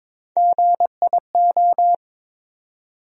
Morse code, heard as a keyed tone, about 22 wpm, 700 Hz. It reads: GIO
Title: USA Call Sign Postfixes - 3 Letters 22wpm